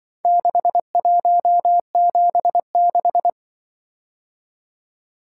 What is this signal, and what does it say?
Morse code, heard as a keyed tone, about 25 wpm, 700 Hz. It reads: 6176
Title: Speed-Racing - Four Letter-Number Combinations - 20wpm